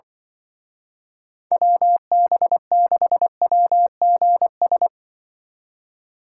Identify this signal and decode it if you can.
Morse code, heard as a keyed tone, about 24 wpm, 700 Hz. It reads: WB6WGS
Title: Speed Racing - USA 2x3 Call Signs 20wpm